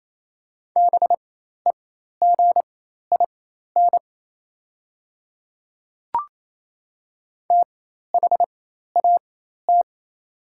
Morse code, heard as a keyed tone, about 28 wpm, spaced as for 13 wpm, 700 Hz. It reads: BEGIN THAT
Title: 4x Character Spacing - Top 300 Words - 28wpm